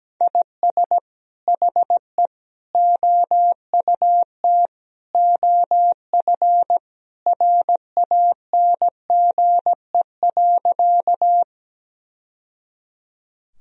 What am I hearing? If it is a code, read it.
Morse code, heard as a keyed tone, about 17 wpm, 700 Hz. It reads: IS HE OUT OF RANGE.
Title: Sentences from Top 1000 Words - Review Part 4 - 17wpm